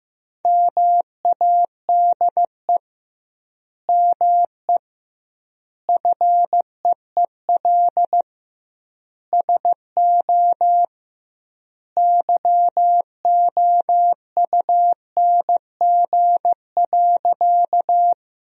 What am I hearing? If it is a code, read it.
Morse code, heard as a keyed tone, about 16 wpm, 700 Hz. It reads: MADE ME FEEL SO YOUNG.
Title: 2x Word-Spacing - Sentences from Top 400 Words - 15wpm